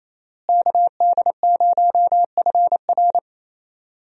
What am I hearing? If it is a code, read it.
Morse code, heard as a keyed tone, about 28 wpm, 700 Hz. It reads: KD0FR
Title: USA Call Signs - 5 Characters - Repeated 3x - 28wpm